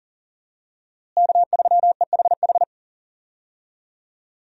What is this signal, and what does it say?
Morse code, heard as a keyed tone, about 40 wpm, 700 Hz. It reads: K3EHH
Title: USA Call Signs - 5 Characters - Repeated 3x - 40wpm